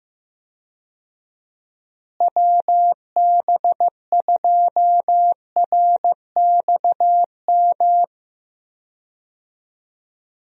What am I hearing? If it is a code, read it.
Morse code, heard as a keyed tone, about 15 wpm, 700 Hz. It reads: WB2RXM
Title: USA Call Signs - 6 Characters - Repeated 2x - 15wpm